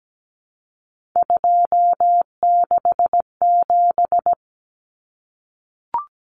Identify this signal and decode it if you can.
Morse code, heard as a keyed tone, about 17 wpm, 700 Hz. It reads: 267
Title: Common Cut Numbers - 3-Digit 17wpm